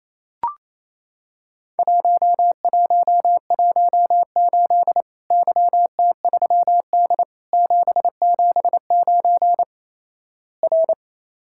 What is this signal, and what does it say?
Morse code, heard as a keyed tone, about 28 wpm, 700 Hz. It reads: 1118 YT3D 779 R
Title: Worked All Europe DX Contest - QTC Exchange - 28wpm